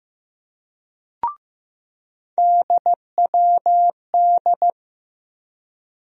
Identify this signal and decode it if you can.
Morse code, heard as a keyed tone, about 15 wpm, 700 Hz. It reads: DWD
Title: USA Call Sign Postfixes - 3 Letters 15wpm